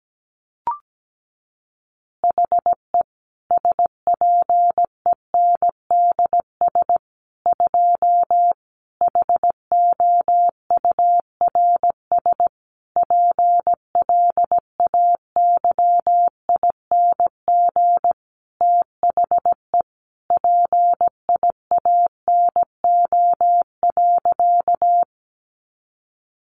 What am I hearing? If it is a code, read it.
Morse code, heard as a keyed tone, about 17 wpm, 700 Hz. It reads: HE SPENDS 2 HOURS PLAYING THE PIANO.
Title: Additional Easy Phrases 17wpm